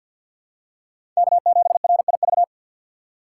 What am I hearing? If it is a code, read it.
Morse code, heard as a keyed tone, about 50 wpm, 700 Hz. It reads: K7RIV
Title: USA Call Signs - 5 Characters - 3x Character Spacing - 50wpm